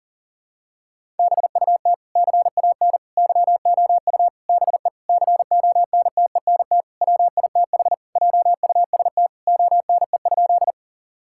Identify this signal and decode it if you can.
Morse code, heard as a keyed tone, about 40 wpm, 700 Hz. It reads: BUT CAN YOU BE CONTENT WITH JUST ONE?
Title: Sentences from Top 1000 Words - Review Part 6 - 40wpm